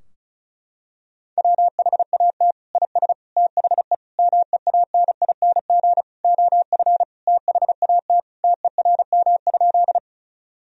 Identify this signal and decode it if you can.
Morse code, heard as a keyed tone, about 34 wpm, 700 Hz. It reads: WHAT IS THE MEANING OF THAT TERM?
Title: Sentences from Top 1000 Words - Review Part 4 - 35wpm